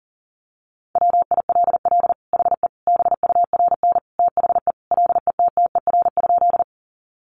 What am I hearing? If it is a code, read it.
Morse code, heard as a keyed tone, about 40 wpm, 700 Hz. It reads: WILL HE BURN THE LETTER?
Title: Sentences from Top 1300 Words 40wpm